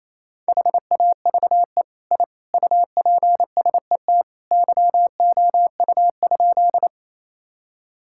Morse code, heard as a keyed tone, about 28 wpm, 700 Hz. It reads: HAVE I UPSET YOU?
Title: Sentences from Top 1000 Words - Review Part 5 - 28wpm